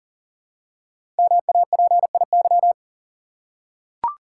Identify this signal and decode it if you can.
Morse code, heard as a keyed tone, about 41 wpm, 700 Hz. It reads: MAPIY
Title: Code Groups - 5 Letters - 40wpm